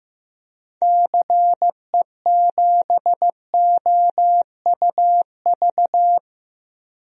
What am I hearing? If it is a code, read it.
Morse code, heard as a keyed tone, about 16 wpm, 700 Hz. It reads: CE7OUV
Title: DXCC-100 Call Signs - 6 Characters - Repeated 3x - 15wpm